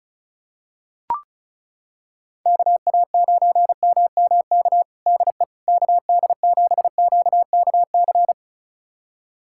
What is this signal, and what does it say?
Morse code, heard as a keyed tone, about 36 wpm, 700 Hz. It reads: KA9MMK DE KD7QKC
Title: USA Callsign DE Callsign 35wpm